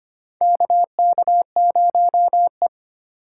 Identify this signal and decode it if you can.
Morse code, heard as a keyed tone, about 25 wpm, 700 Hz. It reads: KK0E
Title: USA Call Signs - 4 Characters - 3x Character Spacing - 25wpm